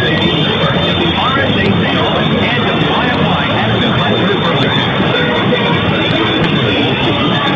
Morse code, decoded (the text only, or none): none